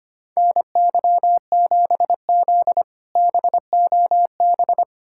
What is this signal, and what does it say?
Morse code, heard as a keyed tone, about 25 wpm, 700 Hz. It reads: NY7Z BOB
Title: K1USN SST QSO Practice - 25wpm